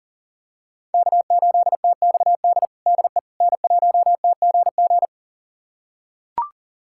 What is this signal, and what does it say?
Morse code, heard as a keyed tone, about 40 wpm, 700 Hz. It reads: K8TXD DE N1TGG
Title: USA Callsign DE Callsign - 5 Characters - 40wpm